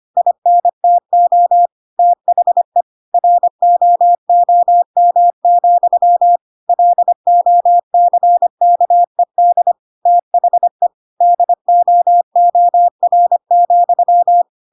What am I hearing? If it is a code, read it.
Morse code, heard as a keyed tone, about 25 wpm, 700 Hz. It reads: INTO THE ROOM, LOCKED THE DOOR,
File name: Pinocchio / Pinocchio ch4 25 25 ik1hge